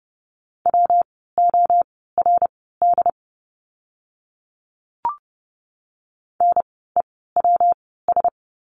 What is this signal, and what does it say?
Morse code, heard as a keyed tone, about 30 wpm, spaced as for 17 wpm, 700 Hz. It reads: WORD NEWS
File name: course / 30:10wpm / 115 Lesson 19f - Letter w - Words 30wpm